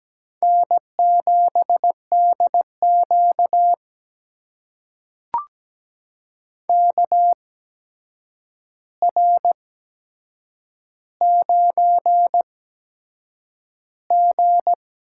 Morse code, heard as a keyed tone, about 17 wpm, spaced as for 7 wpm, 700 Hz. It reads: N7DQ KR9G
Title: USA Call Signs - 8x Character Spacing - 17wpm